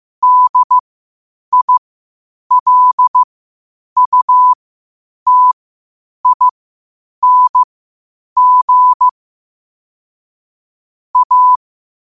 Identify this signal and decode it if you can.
Morse code, heard as a keyed tone, about 15 wpm, spaced as for 8 wpm, 1000 Hz. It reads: DILUTING A
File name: qotdfiles / 8wpm